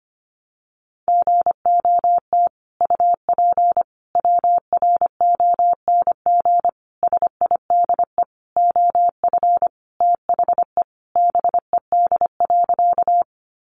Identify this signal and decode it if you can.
Morse code, heard as a keyed tone, about 25 wpm, 700 Hz. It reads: GOT UP WRONG SIDE OF THE BED.
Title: Sentences from Top 900 Words 25wpm